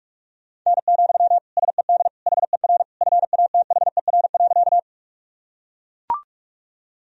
Morse code, heard as a keyed tone, about 42 wpm, 700 Hz. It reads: N, SED HER FATHER.
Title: Sentences from Top 300 Words and Abbreviations 45wpm